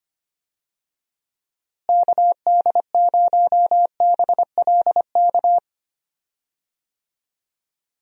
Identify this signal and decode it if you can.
Morse code, heard as a keyed tone, about 25 wpm, 700 Hz. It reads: KD0BLK
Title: USA Call Signs - 6 Characters - Repeated 3x - 25wpm